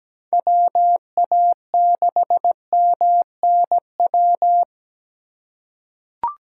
Morse code, heard as a keyed tone, about 17 wpm, 700 Hz. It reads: WA6MNW
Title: USA Call Signs - 3x Character Spacing - 17wpm